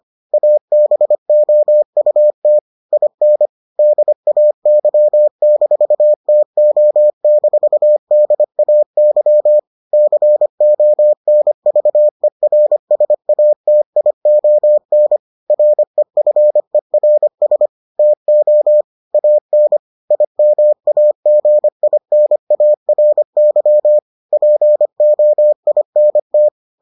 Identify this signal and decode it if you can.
Morse code, heard as a keyed tone, about 25 wpm, 600 Hz. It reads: ABOUT IN DAY-TO-DAY CONVERSATION REFERS TO AN IMAGINARY POINT